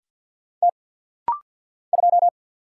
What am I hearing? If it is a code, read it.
Morse code, heard as a keyed tone, about 50 wpm, 700 Hz. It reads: T 2
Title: Single Letter-Number - Rapid-Fire 50wpm